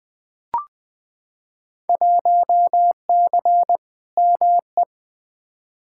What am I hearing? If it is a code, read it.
Morse code, heard as a keyed tone, about 20 wpm, 700 Hz. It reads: 1C ME
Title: Field Day Exchanges - 20wpm